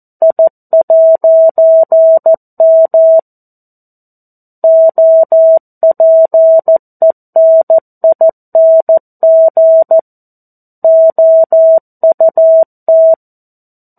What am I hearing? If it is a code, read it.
Morse code, heard as a keyed tone, about 14 wpm, 650 Hz. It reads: I'M OPENING OUT
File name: 02-Chapter-02 14 WPM